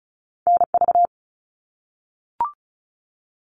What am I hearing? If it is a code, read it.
Morse code, heard as a keyed tone, about 34 wpm, 700 Hz. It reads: NV
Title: US States Repeated 3x 35wpm